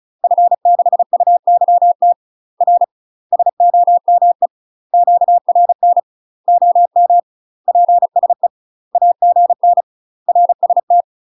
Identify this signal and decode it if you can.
Morse code, heard as a keyed tone, about 36 wpm, 700 Hz. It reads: F6UYT R SOME QRN OM PSE AGN RST
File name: UK MASTER QSOs TRAINING / 35WPM / UK MASTER QSOs 0007